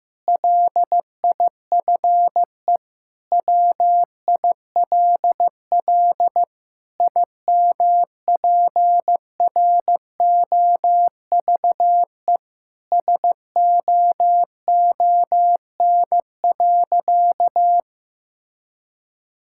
Morse code, heard as a keyed tone, about 15 wpm, 700 Hz. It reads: LIFE WILL IMPROVE SOON.
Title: Sentences from Top 1000 Words - Review Part 5 - 15wpm